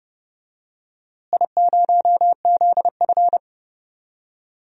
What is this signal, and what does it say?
Morse code, heard as a keyed tone, about 30 wpm, 700 Hz. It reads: I0ZF
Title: DXCC-100 Call Signs - 4 Characters - Repeated 3x - 30wpm